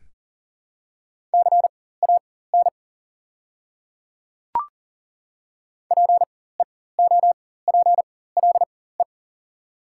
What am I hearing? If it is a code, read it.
Morse code, heard as a keyed tone, about 40 wpm, spaced as for 19 wpm, 700 Hz. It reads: CAN PEOPLE
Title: 4x Character Spacing - Top 200 Words - 40wpm